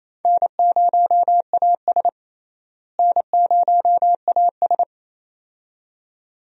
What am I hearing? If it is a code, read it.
Morse code, heard as a keyed tone, about 28 wpm, 700 Hz. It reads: N0AS N0AS
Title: USA Call Signs - 4 Characters - Repeated 2x - 28wpm